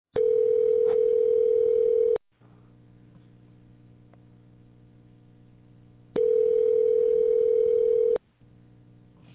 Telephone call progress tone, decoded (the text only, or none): ringback tone